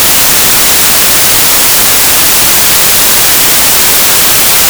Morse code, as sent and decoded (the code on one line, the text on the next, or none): none